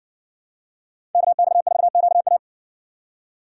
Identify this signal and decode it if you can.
Morse code, heard as a keyed tone, about 61 wpm, 700 Hz. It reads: KX4QA